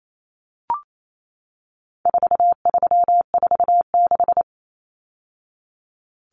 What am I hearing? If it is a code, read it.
Morse code, heard as a keyed tone, about 28 wpm, 700 Hz. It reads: 4346
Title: Numbers - 4-Digit 28wpm